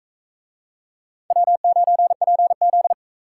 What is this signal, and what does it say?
Morse code, heard as a keyed tone, about 40 wpm, 700 Hz. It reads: W9PZ